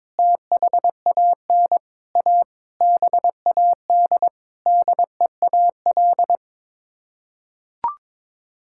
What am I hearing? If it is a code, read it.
Morse code, heard as a keyed tone, about 22 wpm, 700 Hz. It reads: THAN A BAD DEAL
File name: practice / 22wpm / Sets of 8 Words - 22wpm